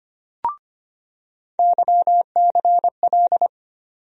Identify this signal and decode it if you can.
Morse code, heard as a keyed tone, about 25 wpm, 700 Hz. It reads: YCL
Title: USA Call Sign Postfixes - 3 Letters 25wpm